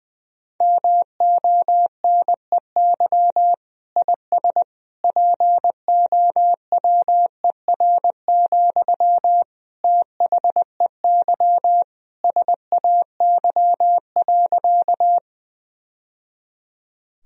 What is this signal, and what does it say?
Morse code, heard as a keyed tone, about 20 wpm, 700 Hz. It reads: MONEY IS POWER, THEY SAY.
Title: Sentences from Top 1000 Words - Review Part 2 - 20wpm